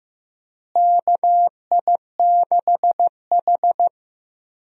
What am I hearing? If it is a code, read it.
Morse code, heard as a keyed tone, about 15 wpm, 700 Hz. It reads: KI6H
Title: USA Call Signs - 4 Characters - Repeated 3x - 15wpm